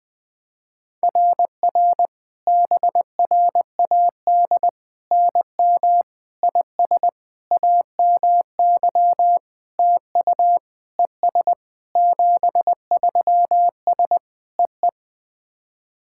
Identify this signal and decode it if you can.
Morse code, heard as a keyed tone, about 19 wpm, 700 Hz. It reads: RR BRAD NM IS AMY TU ES 73S EE
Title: SOTA QSO Practice 20wpm